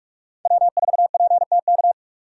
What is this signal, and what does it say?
Morse code, heard as a keyed tone, about 44 wpm, 700 Hz. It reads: W4PTK